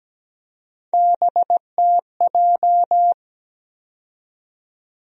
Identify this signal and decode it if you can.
Morse code, heard as a keyed tone, about 17 wpm, 700 Hz. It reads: BTJ
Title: USA Call Sign Postfixes - 3 Letters 17wpm